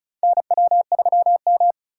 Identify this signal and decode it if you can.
Morse code, heard as a keyed tone, about 34 wpm, 700 Hz. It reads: NW3M
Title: USA Call Signs - 4 Characters - 4x Character Spacing - 35wpm